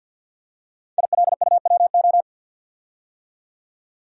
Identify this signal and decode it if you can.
Morse code, heard as a keyed tone, about 50 wpm, 700 Hz. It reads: ILUWO
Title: Code Groups - 5 Letters - 50wpm